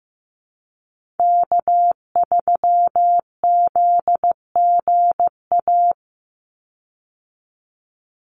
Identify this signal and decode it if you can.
Morse code, heard as a keyed tone, about 15 wpm, 700 Hz. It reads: K3ZGA
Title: Call Signs Repeated 3x - 15wpm